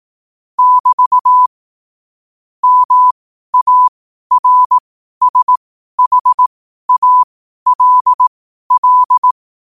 Morse code, heard as a keyed tone, about 18 wpm, spaced as for 12 wpm, 1000 Hz. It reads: <BT> MARSHALL
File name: 12wpm